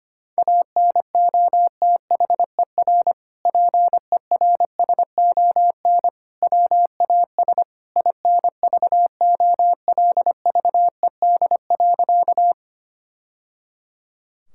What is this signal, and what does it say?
Morse code, heard as a keyed tone, about 25 wpm, 700 Hz. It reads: ANOTHER PERSON WAS INVOLVED.
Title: Sentences from Top 1000 Words - Review Part 5 - 25wpm